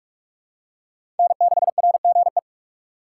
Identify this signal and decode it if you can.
Morse code, heard as a keyed tone, about 44 wpm, 700 Hz. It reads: N6RGE
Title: USA Call Signs - 6x Character Spacing - 45wpm